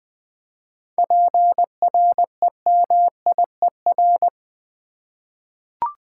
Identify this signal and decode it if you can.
Morse code, heard as a keyed tone, about 21 wpm, 700 Hz. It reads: PREMIER
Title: 7 Letter Words - First 3 Letters Given - 20wpm